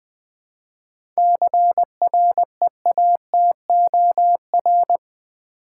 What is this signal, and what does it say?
Morse code, heard as a keyed tone, about 20 wpm, 700 Hz. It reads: CREATOR
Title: 7 Letter Words - First 3 Letters Given - 20wpm